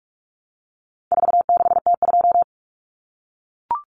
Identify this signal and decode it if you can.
Morse code, heard as a keyed tone, about 44 wpm, 700 Hz. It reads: 46T2
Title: Common Cut Numbers - 4-Digit 45wpm